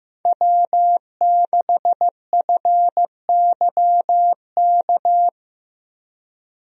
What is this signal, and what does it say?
Morse code, heard as a keyed tone, about 15 wpm, 700 Hz. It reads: W6FYK
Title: USA Call Signs - 5 Characters - Repeated 2x - 15wpm